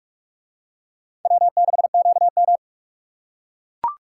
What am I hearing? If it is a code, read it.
Morse code, heard as a keyed tone, about 41 wpm, 700 Hz. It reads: W6QM